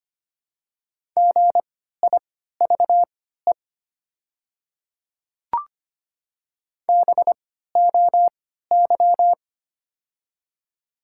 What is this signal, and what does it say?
Morse code, heard as a keyed tone, about 25 wpm, spaced as for 14 wpm, 700 Hz. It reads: GIVE BOY
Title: 3x Character Spacing - Top 200 Words - 25wpm